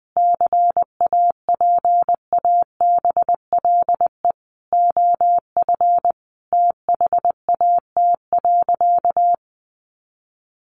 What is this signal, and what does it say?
Morse code, heard as a keyed tone, about 20 wpm, 700 Hz. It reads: CAPABLE OF THAT.
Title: Sentences from Top 1200 Words 20wpm